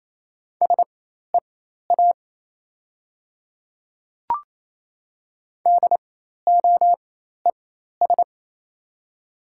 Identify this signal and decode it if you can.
Morse code, heard as a keyed tone, about 28 wpm, spaced as for 13 wpm, 700 Hz. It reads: SEA DOES
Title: 4x Character Spacing - Top 300 Words - 28wpm